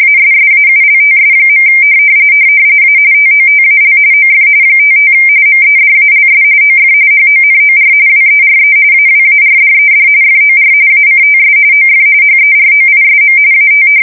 RTTY, about 45.45 baud. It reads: ASSAULTS. A TOTAL OF 55 MEDALS OF HONOR WERE AWARDED FOR ACTS OF HEROISM AT
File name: fsk170 / usnavy4